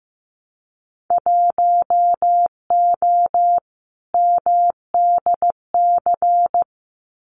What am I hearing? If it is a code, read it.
Morse code, heard as a keyed tone, about 15 wpm, 700 Hz. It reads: 1O MDC
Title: Winter Field Day Exchanges 15wpm